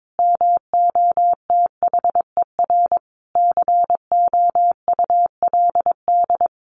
Morse code, heard as a keyed tone, about 22 wpm, 700 Hz. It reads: MOTHER COULD
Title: Sentences from Top 1000 Words 22wpm